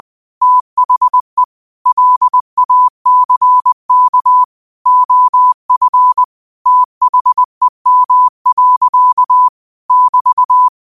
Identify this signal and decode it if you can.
Morse code, heard as a keyed tone, about 20 wpm, 1000 Hz. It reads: THE LACK OF THEM. <BT>
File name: qotdfiles / 20wpm